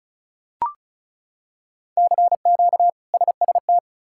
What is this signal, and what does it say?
Morse code, heard as a keyed tone, about 34 wpm, 700 Hz. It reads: CQ SST
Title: K1USN SST QSO Practice - 35wpm